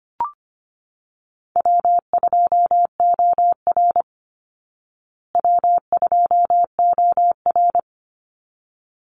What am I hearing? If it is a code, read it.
Morse code, heard as a keyed tone, about 25 wpm, 700 Hz. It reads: W2OR W2OR
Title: Easy Call Signs Repeated 2x - 25wpm